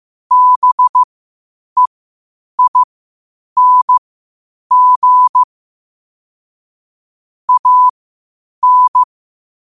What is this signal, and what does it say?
Morse code, heard as a keyed tone, about 15 wpm, spaced as for 8 wpm, 1000 Hz. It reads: BEING AN
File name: qotdfiles / 8wpm